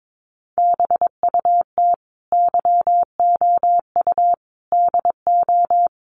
Morse code, heard as a keyed tone, about 22 wpm, 700 Hz. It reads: BUT YOU DO
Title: Sentences from Top 600 Words 22wpm